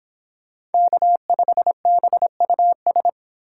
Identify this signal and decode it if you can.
Morse code, heard as a keyed tone, about 31 wpm, 700 Hz. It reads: K5BUS